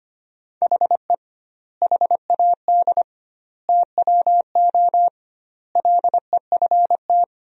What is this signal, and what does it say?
Morse code, heard as a keyed tone, about 25 wpm, 700 Hz. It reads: HE HAD TWO LEFT
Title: 2x Word-Spacing - Sentences from Top 300 Words - 25wpm